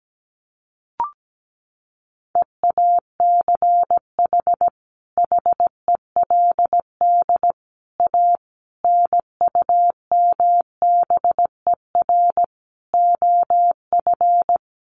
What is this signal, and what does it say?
Morse code, heard as a keyed tone, about 17 wpm, 700 Hz. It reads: EACH HELD A NUMBER OF
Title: A Sound Of Thunder 17wpm